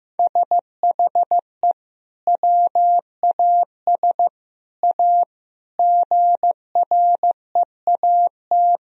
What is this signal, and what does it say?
Morse code, heard as a keyed tone, about 15 wpm, 700 Hz. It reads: SHE WAS A GREAT
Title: Sentences from Top 1000 Words - Review Part 4 - 15wpm